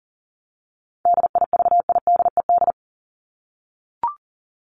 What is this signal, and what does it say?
Morse code, heard as a keyed tone, about 39 wpm, 700 Hz. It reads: DIVIDED
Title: ED Suffix - Top 100 Words with Base Words 40wpm